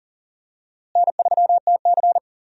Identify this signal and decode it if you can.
Morse code, heard as a keyed tone, about 40 wpm, 700 Hz. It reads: N3TC
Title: USA Call Signs - 4 Characters - Repeated 3x - 40wpm